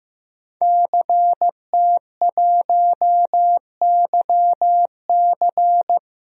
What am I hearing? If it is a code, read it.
Morse code, heard as a keyed tone, about 15 wpm, 700 Hz. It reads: CT1YC
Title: DXCC-100 Call Signs - 5 Characters - 15wpm